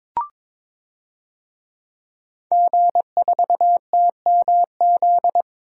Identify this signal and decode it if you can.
Morse code, heard as a keyed tone, about 22 wpm, 700 Hz. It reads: G4TMZ
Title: DXCC-100 Call Signs - 5 Characters - 22wpm